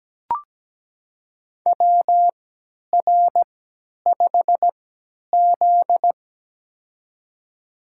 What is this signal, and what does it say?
Morse code, heard as a keyed tone, about 17 wpm, spaced as for 10 wpm, 700 Hz. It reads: WR5Z
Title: USA Call Signs - 3x Character Spacing - 17wpm